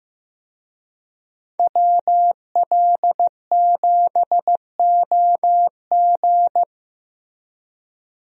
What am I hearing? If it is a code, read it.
Morse code, heard as a keyed tone, about 16 wpm, 700 Hz. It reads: WL7OG
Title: USA Call Signs - 5 Characters - Repeated 2x - 15wpm